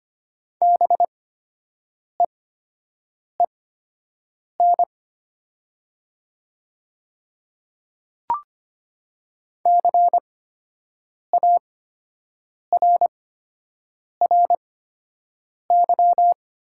Morse code, heard as a keyed tone, about 25 wpm, spaced as for 7 wpm, 700 Hz. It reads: BEEN CARRY
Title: 8x Character Spacing - Top 300 Words - 25wpm